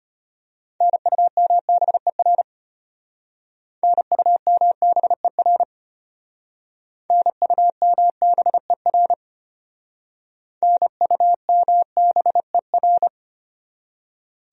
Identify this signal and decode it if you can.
Morse code, heard as a keyed tone, about 34 wpm, 700 Hz. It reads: NUMBER NUMBER NUMBER NUMBER